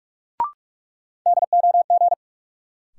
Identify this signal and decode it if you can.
Morse code, heard as a keyed tone, about 46 wpm, 700 Hz. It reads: DOG
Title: Sentences from Top 400 Words - New Word Drill - 45wpm